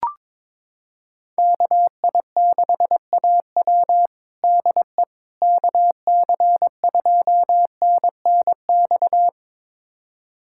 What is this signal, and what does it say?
Morse code, heard as a keyed tone, about 22 wpm, 700 Hz. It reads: KI6AW DE KC2NNX
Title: USA Callsign DE Callsign 22wpm